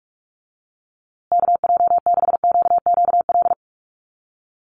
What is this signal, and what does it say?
Morse code, heard as a keyed tone, about 44 wpm, 700 Hz. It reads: KJ6QQL
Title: USA Call Signs Repeated 3x 45wpm